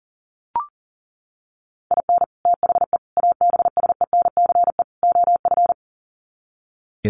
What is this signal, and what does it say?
Morse code, heard as a keyed tone, about 40 wpm, 700 Hz. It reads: IN THE ABSENCE OF
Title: Sets of 4 Words 40wpm